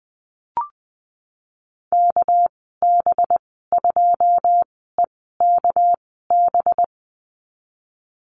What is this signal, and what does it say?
Morse code, heard as a keyed tone, about 20 wpm, spaced as for 15 wpm, 700 Hz. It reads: KB2EKB
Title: 219 Lesson 35g - Number 0 - Call Signs 20wpm